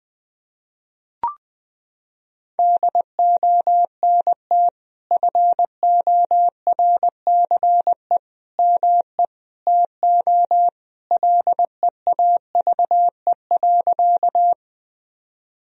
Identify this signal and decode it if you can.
Morse code, heard as a keyed tone, about 20 wpm, 700 Hz. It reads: DONT FORCE ME TO LEAVE.
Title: Sentences from Top 1000 Words - Review Part 3 - 20wpm